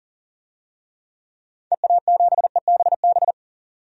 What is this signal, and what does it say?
Morse code, heard as a keyed tone, about 40 wpm, 700 Hz. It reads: EA7EBB
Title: DXCC-100 Call Signs - 6 Characters - 40wpm